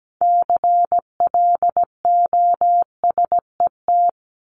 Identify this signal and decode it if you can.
Morse code, heard as a keyed tone, about 17 wpm, 700 Hz. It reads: CLOSET
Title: Top 1751-2000 Words 17wpm